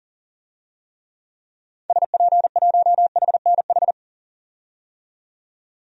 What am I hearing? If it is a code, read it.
Morse code, heard as a keyed tone, about 40 wpm, 700 Hz. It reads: SP1HNH
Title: DXCC-100 Call Signs - 6 Characters - 40wpm